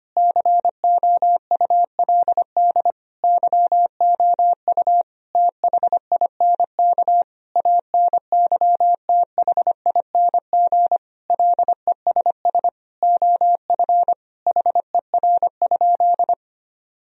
Morse code, heard as a keyed tone, about 25 wpm, 700 Hz. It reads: COULD YOU THINK ANYTHING LESS OF HER?
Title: Sentences from Top 1000 Words - Review Part 2 - 25wpm